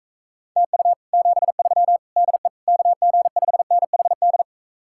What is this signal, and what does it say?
Morse code, heard as a keyed tone, about 45 wpm, 700 Hz. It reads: TU 73 DE KG5NHD